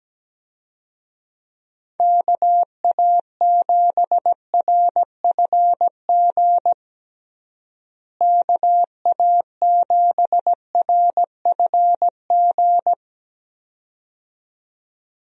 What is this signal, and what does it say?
Morse code, heard as a keyed tone, about 17 wpm, 700 Hz. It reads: KA7RFG KA7RFG
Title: USA Call Signs - 6 Characters - Repeated 2x - 17wpm